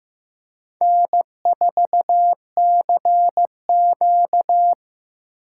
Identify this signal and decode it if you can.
Morse code, heard as a keyed tone, about 18 wpm, 700 Hz. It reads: N4CQ
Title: Speed Racing - USA Call Signs 15wpm